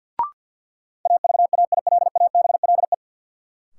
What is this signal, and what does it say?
Morse code, heard as a keyed tone, about 47 wpm, 700 Hz. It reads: AVAILABLE
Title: Sentences from Top 600 Words - New Word Drill - 50wpm